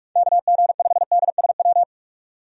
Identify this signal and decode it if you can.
Morse code, heard as a keyed tone, about 47 wpm, 700 Hz. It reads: KG5DSW
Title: USA Call Signs - 6 Characters - 45wpm